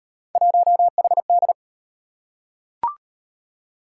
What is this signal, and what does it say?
Morse code, heard as a keyed tone, about 36 wpm, 700 Hz. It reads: 1HD